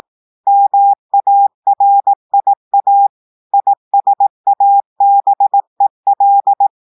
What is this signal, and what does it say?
Morse code, heard as a keyed tone, about 18 wpm, 800 Hz. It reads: MARIA ISABEL
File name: audios / portugais / portugais texte en clair 1 vitesse 18 mots minutes